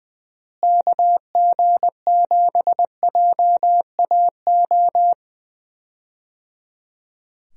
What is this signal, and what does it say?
Morse code, heard as a keyed tone, about 20 wpm, 700 Hz. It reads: KG7JAO
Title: USA Call Signs - 6 Characters - Repeated 3x - 20wpm